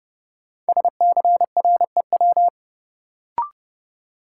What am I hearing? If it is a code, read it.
Morse code, heard as a keyed tone, about 30 wpm, 700 Hz. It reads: SCREW
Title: Sentences from Top 1600 Words - New Word Drill Alt - 30wpm